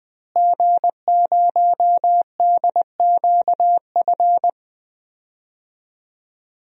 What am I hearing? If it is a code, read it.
Morse code, heard as a keyed tone, about 20 wpm, 700 Hz. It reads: G0DQF
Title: DXCC-100 Call Signs - 5 Characters - Repeated 3x - 20wpm